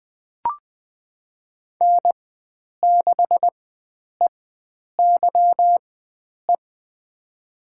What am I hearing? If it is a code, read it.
Morse code, heard as a keyed tone, about 20 wpm, spaced as for 9 wpm, 700 Hz. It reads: N6EYE
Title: USA Call Signs - 4x Character Spacing - 20wpm